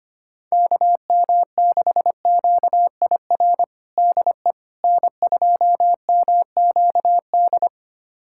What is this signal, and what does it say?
Morse code, heard as a keyed tone, about 25 wpm, 700 Hz. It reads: KM6QIR DE N2MQD
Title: USA Callsign DE Callsign 25wpm